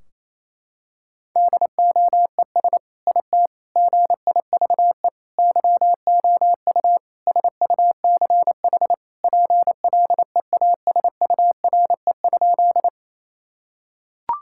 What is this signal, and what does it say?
Morse code, heard as a keyed tone, about 28 wpm, 700 Hz. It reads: DOES IT GIVE YOU SUCH PLEASURE?
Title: Sentences from Top 1600 Words 28wpm